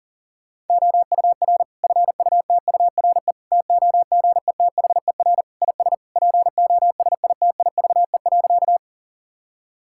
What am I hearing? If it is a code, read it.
Morse code, heard as a keyed tone, about 40 wpm, 700 Hz. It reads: OUR FUTURE TOGETHER IS POSITIVE.
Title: Sentences from Top 1000 Words - Review Part 5 - 40wpm